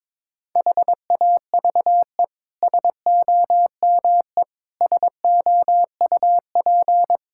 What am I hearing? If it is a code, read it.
Morse code, heard as a keyed tone, about 22 wpm, 700 Hz. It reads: HAVE SOME SOUP
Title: Sentences from Top 1000 Words - Review Part 6 - 22wpm